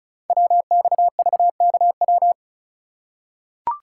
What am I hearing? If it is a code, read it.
Morse code, heard as a keyed tone, about 36 wpm, 700 Hz. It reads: WXVKW
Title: Code Groups - 5 Letters - 35wpm